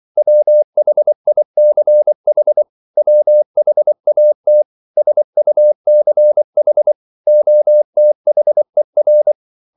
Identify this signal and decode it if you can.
Morse code, heard as a keyed tone, about 24 wpm, 600 Hz. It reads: WHICH WHAT SUCH OTHER
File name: CW / Practice / 500MostCommonEnglishWords24